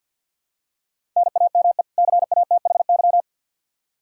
Morse code, heard as a keyed tone, about 51 wpm, 700 Hz. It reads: NAME CATHY
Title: Practice QSO - Exchange 1 - Beginning - 50wpm